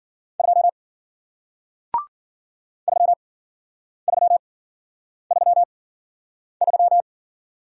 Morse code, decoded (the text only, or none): 2 3333